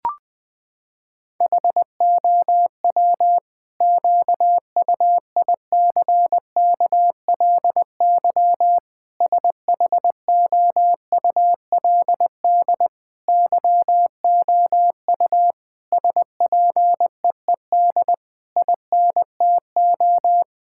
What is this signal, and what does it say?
Morse code, heard as a keyed tone, about 20 wpm, 700 Hz. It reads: HOW QUICKLY SHOULD YOU SPEED INTO
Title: Sentences from Top 1000 Words - Review Part 5 - 20wpm